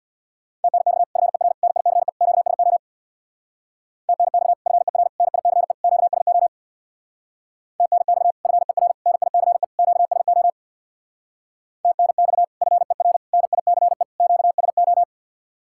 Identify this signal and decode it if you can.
Morse code, heard as a keyed tone, about 65 wpm, 700 Hz. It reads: TNX FER NICE QSO TNX FER NICE QSO TNX FER NICE QSO TNX FER NICE QSO